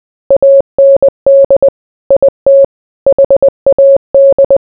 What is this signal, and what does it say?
Morse code, heard as a keyed tone, about 20 wpm, 550 Hz. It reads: AND IT HAD